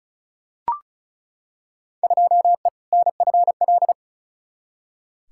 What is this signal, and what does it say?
Morse code, heard as a keyed tone, about 34 wpm, 700 Hz. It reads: 2E NFL
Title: Field Day Exchanges Encore - 35wpm